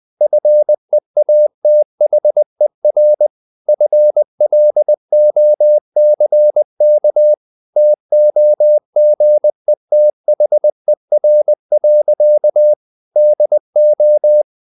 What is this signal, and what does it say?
Morse code, heard as a keyed tone, about 20 wpm, 600 Hz. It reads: FEATHER FLOCK TOGETHER. DO
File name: CW / Practice / EnglishSayings20